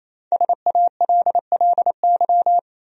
Morse code, heard as a keyed tone, about 28 wpm, 700 Hz. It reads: SALLY
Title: USA Common Names - 28wpm